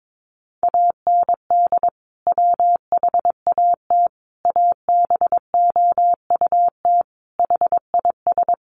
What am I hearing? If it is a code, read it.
Morse code, heard as a keyed tone, about 22 wpm, 700 Hz. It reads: AND WHAT ABOUT HIS
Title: Sentences from Top 1200 Words 22wpm